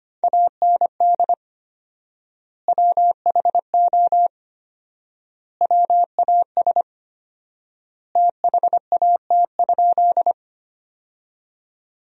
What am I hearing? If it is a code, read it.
Morse code, heard as a keyed tone, about 25 wpm, 700 Hz. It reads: AND WHO WAS THAT?
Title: 4x Word-Spacing - Sentences from Top 100 Words - 25wpm